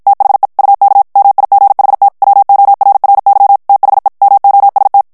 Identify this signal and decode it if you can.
Morse code, heard as a keyed tone, about 50 wpm, 800 Hz. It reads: THE UK MIGHT WORRY THE MOST
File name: morse mp3 / ANGLAIS texte clair 1 vitesse 50 mots minute